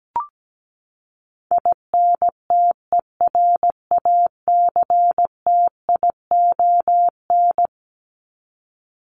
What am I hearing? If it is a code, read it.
Morse code, heard as a keyed tone, about 17 wpm, 700 Hz. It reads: INTERACTION
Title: Top 1251-1500 Words 17wpm